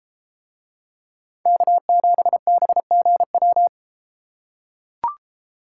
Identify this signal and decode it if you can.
Morse code, heard as a keyed tone, about 32 wpm, 700 Hz. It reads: K7BGW